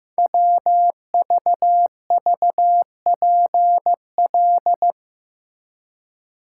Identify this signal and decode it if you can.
Morse code, heard as a keyed tone, about 15 wpm, 700 Hz. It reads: WVVPL
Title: Code Groups - 5 Letter:Numbers - 15wpm